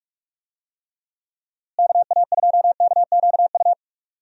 Morse code, heard as a keyed tone, about 46 wpm, 700 Hz. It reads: KA2KQU
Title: USA Call Signs - 6 Characters - Repeated 2x - 45wpm